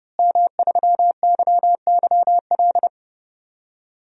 Morse code, heard as a keyed tone, about 32 wpm, 700 Hz. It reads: M3YYL
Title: DXCC-100 Call Signs - 5 Characters - Repeated 2x - 30wpm